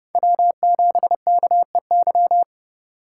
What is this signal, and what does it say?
Morse code, heard as a keyed tone, about 31 wpm, 700 Hz. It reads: W7KEY
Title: USA Call Signs - 2x Character Spacing - 30wpm